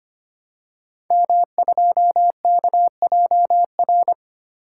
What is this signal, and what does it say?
Morse code, heard as a keyed tone, about 25 wpm, 700 Hz. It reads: M2KJR
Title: Code Groups - 5 Letter:Numbers - 25wpm